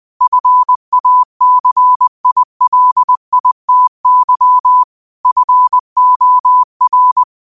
Morse code, heard as a keyed tone, about 20 wpm, 1000 Hz. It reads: FACILITY FOR
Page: New quotes every day in morse code at 20 Words per minute.